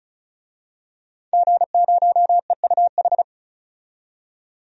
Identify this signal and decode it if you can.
Morse code, heard as a keyed tone, about 34 wpm, 700 Hz. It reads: G0EUH
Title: DXCC-100 Call Signs - 5 Characters - 35wpm